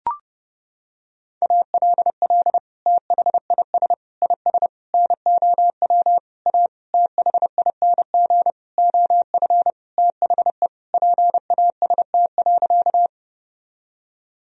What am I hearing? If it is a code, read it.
Morse code, heard as a keyed tone, about 32 wpm, 700 Hz. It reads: ALL THIS IS NOW A THING OF THE PAST.
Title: Sentences from Top 1000 Words - Review Part 3 - 30wpm